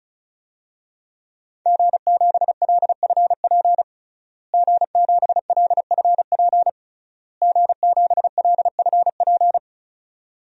Morse code, heard as a keyed tone, about 36 wpm, 700 Hz. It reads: G7LFP G7LFP G7LFP
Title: DXCC-100 Call Signs - 5 Characters - Repeated 3x - 35wpm